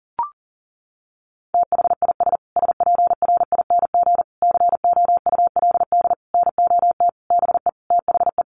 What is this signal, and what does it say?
Morse code, heard as a keyed tone, about 40 wpm, 700 Hz. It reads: THIS SPRING COULD NOT BE THE
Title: Sentences from Top 1500 Words 40wpm